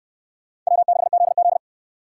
Morse code, heard as a keyed tone, about 62 wpm, 700 Hz. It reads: W6ZZ